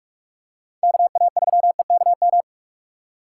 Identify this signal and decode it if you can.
Morse code, heard as a keyed tone, about 46 wpm, 700 Hz. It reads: KA3EKM
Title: USA Call Signs - 6 Characters - 6x Character Spacing - 45wpm